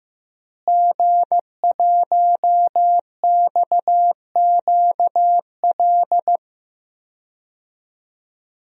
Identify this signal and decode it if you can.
Morse code, heard as a keyed tone, about 15 wpm, 700 Hz. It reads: G1XQL
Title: DXCC-100 Call Signs - 5 Characters - Repeated 3x - 15wpm